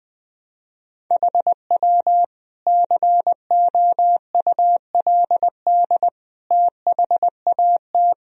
Morse code, heard as a keyed tone, about 21 wpm, 700 Hz. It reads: HW COULD THAT
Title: Sentences from Top 400 Words and Abbreviations 20wpm